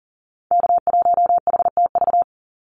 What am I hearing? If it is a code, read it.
Morse code, heard as a keyed tone, about 40 wpm, 700 Hz. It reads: K1HTV
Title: Call Signs - 40wpm